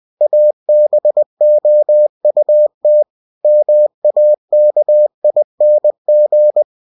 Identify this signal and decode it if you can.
Morse code, heard as a keyed tone, about 20 wpm, 600 Hz. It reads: ABOUT MAKING
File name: cw-books / winnie-the-pooh / winnie-20wpm-page-0072